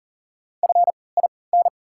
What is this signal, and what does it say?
Morse code, heard as a keyed tone, about 40 wpm, spaced as for 23 wpm, 700 Hz. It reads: FIN
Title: 3x Character Spacing - Top 500 Words - 40wpm